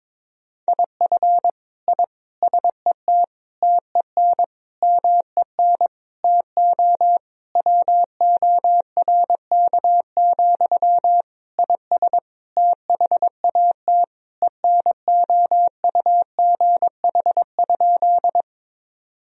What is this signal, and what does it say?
Morse code, heard as a keyed tone, about 22 wpm, 700 Hz. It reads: IF I SET TEN MEN TO WORK, IS THAT ENOUGH?
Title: Sentences from Top 1000 Words - Review Part 2 - 22wpm